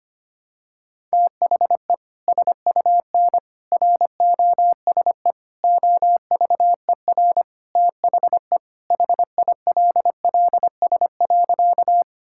Morse code, heard as a keyed tone, about 25 wpm, 700 Hz. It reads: THE SUN ROSE OVER THE HILLS.
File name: practice / 25wpm / Sentences with 2-Word Phrases - 25wpm